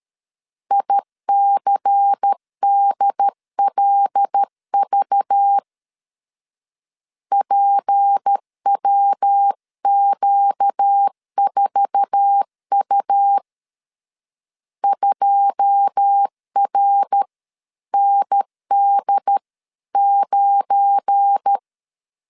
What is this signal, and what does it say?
Morse code, heard as a keyed tone, about 13 wpm, 800 Hz. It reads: ICDLV PWQ4U 2RND9
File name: CW / DET / 1 9 1500C S A 12wpm 3800